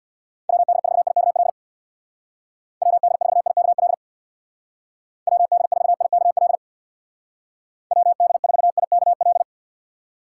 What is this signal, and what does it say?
Morse code, heard as a keyed tone, about 65 wpm, 700 Hz. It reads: WD4IKL WD4IKL WD4IKL WD4IKL